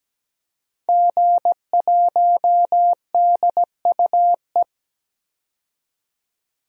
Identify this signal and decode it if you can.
Morse code, heard as a keyed tone, about 17 wpm, 700 Hz. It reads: G1DUE
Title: DXCC-100 Call Signs - 5 Characters - 17wpm